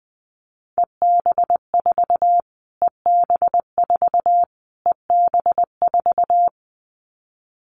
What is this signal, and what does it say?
Morse code, heard as a keyed tone, about 20 wpm, 700 Hz. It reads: EB4 EB4 EB4
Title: Callsign Prefixes for Top 25 DXCC - Repeated 3x - 20wpm